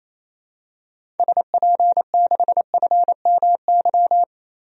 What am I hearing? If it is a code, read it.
Morse code, heard as a keyed tone, about 28 wpm, 700 Hz. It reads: SP6FMY
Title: DXCC-100 Call Signs - 6 Characters - 28wpm